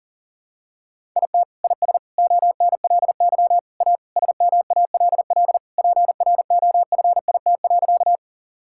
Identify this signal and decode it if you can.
Morse code, heard as a keyed tone, about 40 wpm, 700 Hz. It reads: IT IS ONLY A SMALL PROFIT.
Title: Sentences from Top 1000 Words - Review Part 5 - 40wpm